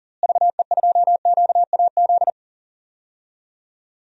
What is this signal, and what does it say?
Morse code, heard as a keyed tone, about 37 wpm, 700 Hz. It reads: VE2QAZ
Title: DXCC-100 Call Signs - 6 Characters - Repeated 2x - 40wpm